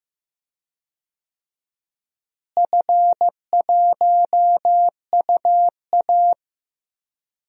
Morse code, heard as a keyed tone, about 15 wpm, 700 Hz. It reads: F1UA
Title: DXCC-100 Call Signs - 4 Characters - Repeated 2x - 15wpm